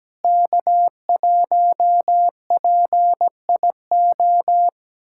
Morse code, heard as a keyed tone, about 17 wpm, 700 Hz. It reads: K1PIO
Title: USA Callsign DE Callsign 17wpm